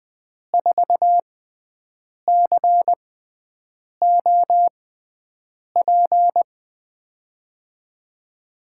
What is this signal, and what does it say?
Morse code, heard as a keyed tone, about 21 wpm, spaced as for 7 wpm, 700 Hz. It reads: 4COP
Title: USA Call Signs - 5 Characters - 6x Character Spacing - 20wpm